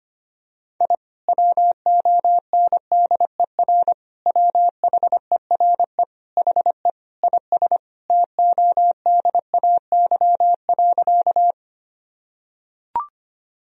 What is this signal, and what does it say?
Morse code, heard as a keyed tone, about 25 wpm, 700 Hz. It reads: I WONDER WHERE HE IS TODAY.
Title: Sentences from Top 1000 Words - Review Part 3 - 25wpm